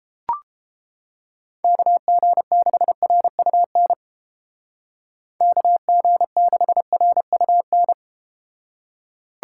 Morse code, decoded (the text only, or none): KG6RUN KG6RUN